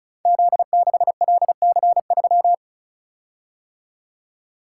Morse code, heard as a keyed tone, about 36 wpm, 700 Hz. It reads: Z6LC3
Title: Code Groups - 5 Letter:Numbers - 35wpm